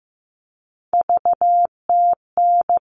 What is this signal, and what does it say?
Morse code, heard as a keyed tone, about 15 wpm, 700 Hz. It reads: VTN
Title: All Cut Numbers - 3-Digit 15wpm